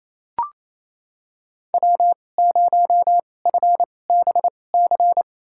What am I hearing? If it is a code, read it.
Morse code, heard as a keyed tone, about 28 wpm, spaced as for 20 wpm, 700 Hz. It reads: W0FBC
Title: USA Call Signs - 2x Character Spacing - 28wpm